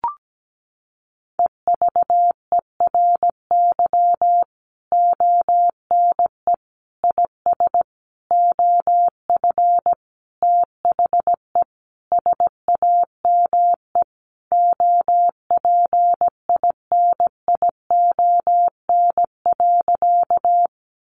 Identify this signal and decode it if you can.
Morse code, heard as a keyed tone, about 17 wpm, 700 Hz. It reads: EVERY ONE IS OF THE SAME OPINION.
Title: Sentences from Top 1200 Words 17wpm